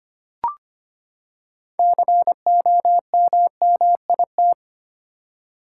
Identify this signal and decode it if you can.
Morse code, heard as a keyed tone, about 25 wpm, 700 Hz. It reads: COMMIT
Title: Six Letter Words - 25wpm